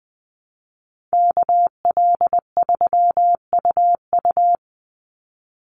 Morse code, heard as a keyed tone, about 20 wpm, 700 Hz. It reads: KL3UU
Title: USA Call Signs 20wpm